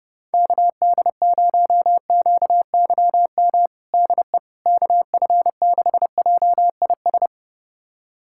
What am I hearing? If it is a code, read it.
Morse code, heard as a keyed tone, about 32 wpm, 700 Hz. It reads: KD0QYM DE KF6JIS
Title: USA Callsign DE Callsign - 6 Characters - 30wpm